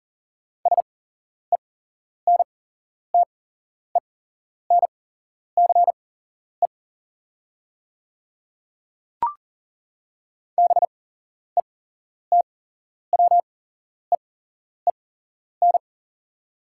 Morse code, heard as a keyed tone, about 41 wpm, spaced as for 11 wpm, 690 Hz. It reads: SENTENCE BETWEEN
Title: 8x Character Spacing - Top 500 Words - 40wpm